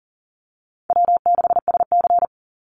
Morse code, heard as a keyed tone, about 40 wpm, 700 Hz. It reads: W6SC
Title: Call Signs Repeated 2x - 40wpm